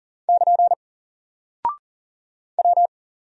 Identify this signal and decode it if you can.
Morse code, heard as a keyed tone, about 40 wpm, 700 Hz. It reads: <KN> W
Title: Single Character (Letter:Number:Punctuation:Prosign) Alt 40wpm